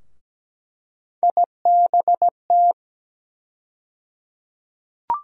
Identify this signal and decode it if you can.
Morse code, heard as a keyed tone, about 17 wpm, 700 Hz. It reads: IBT
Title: USA Call Sign Postfixes - 3 Letters 17wpm